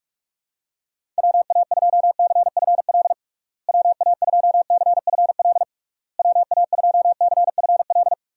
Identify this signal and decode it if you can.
Morse code, heard as a keyed tone, about 46 wpm, 700 Hz. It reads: WA2CFL WA2CFL WA2CFL
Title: USA Call Signs - 6 Characters - Repeated 3x - 45wpm